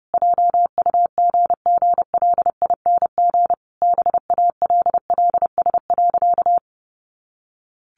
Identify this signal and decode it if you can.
Morse code, heard as a keyed tone, about 30 wpm, 700 Hz. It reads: JUGGLING BALLS.
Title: Easy Phrases 30wpm